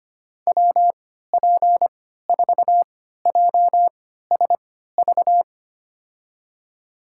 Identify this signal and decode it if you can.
Morse code, heard as a keyed tone, about 25 wpm, spaced as for 14 wpm, 700 Hz. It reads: WP4JSV
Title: USA Call Signs - 6 Characters - 3x Character Spacing - 25wpm